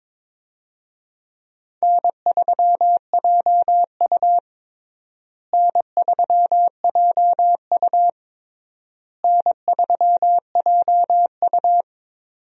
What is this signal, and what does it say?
Morse code, heard as a keyed tone, about 22 wpm, 700 Hz. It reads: N3JU N3JU N3JU
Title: USA Call Signs - 4 Characters - Repeated 3x - 22wpm